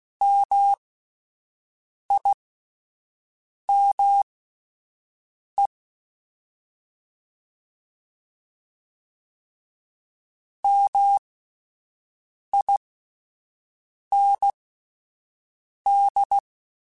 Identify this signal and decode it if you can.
Morse code, heard as a keyed tone, about 16 wpm, spaced as for 5 wpm, 800 Hz. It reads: MIME MIND
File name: Alphabetical / Alphabetical mp3s / Ms @5